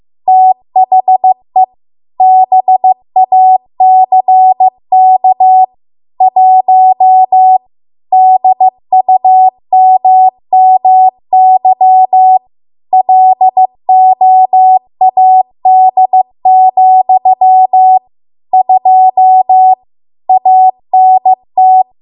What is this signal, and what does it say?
Morse code, heard as a keyed tone, about 15 wpm, 750 Hz. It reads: THE BACK 1 DUMMY LOAD, 2 ANT
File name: Archive / 15 WPM / 15WPM